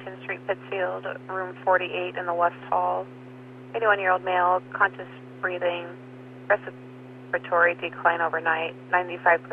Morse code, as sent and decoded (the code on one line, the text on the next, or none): none